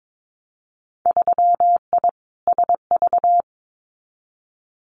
Morse code, heard as a keyed tone, about 22 wpm, 700 Hz. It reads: 3I SV
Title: Winter Field Day Exchanges Encore 22wpm